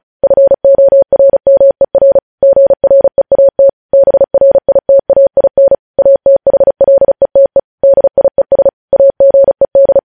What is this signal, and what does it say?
Morse code, heard as a keyed tone, about 36 wpm, 550 Hz. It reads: FORMER GREAT BRITAIN ATHLETE DIES AGED